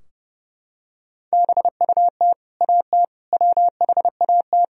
Morse code, heard as a keyed tone, about 32 wpm, 700 Hz. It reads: BUT AT WHAT
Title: Sentences from Top 1000 Words - Review Part 4 - 30wpm